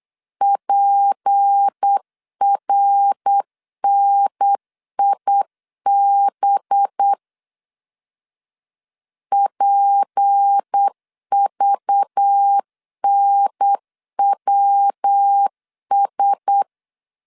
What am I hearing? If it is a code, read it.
Morse code, heard as a keyed tone, about 8 wpm, 800 Hz. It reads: PRNIB PVNWS
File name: LEZIONI / 6 PKX0 / 6 P 8wpm 3025